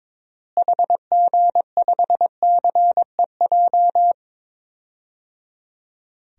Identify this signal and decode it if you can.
Morse code, heard as a keyed tone, about 21 wpm, 700 Hz. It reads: HG5CEJ
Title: DXCC-100 Call Signs - 6 Characters - Repeated 3x - 22wpm